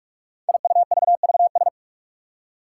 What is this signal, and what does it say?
Morse code, heard as a keyed tone, about 44 wpm, 700 Hz. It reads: IUVVS
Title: Five Challenging Letter-Number Combinations 45wpm